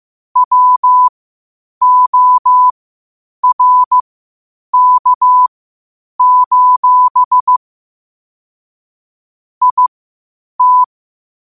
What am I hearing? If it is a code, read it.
Morse code, heard as a keyed tone, about 15 wpm, spaced as for 8 wpm, 1000 Hz. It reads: WORK: IT
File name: qotdfiles / 8wpm